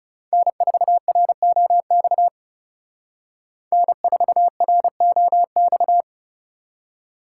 Morse code, heard as a keyed tone, about 38 wpm, 700 Hz. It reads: N4ROX N4ROX